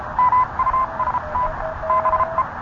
Morse code, heard as a keyed tone, about 30 wpm, 1030 Hz. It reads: MUST BE